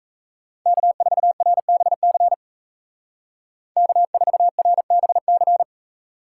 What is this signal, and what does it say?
Morse code, heard as a keyed tone, about 33 wpm, 700 Hz. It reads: K4RBC K4RBC
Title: Speed Racing - USA Call Signs 28wpm